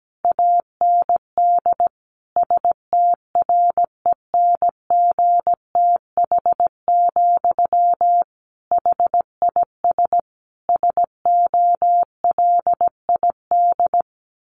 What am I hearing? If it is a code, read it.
Morse code, heard as a keyed tone, about 17 wpm, 700 Hz. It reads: AND STRENGTH, HIS SOLID
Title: Sentences from Top 1500 Words 17wpm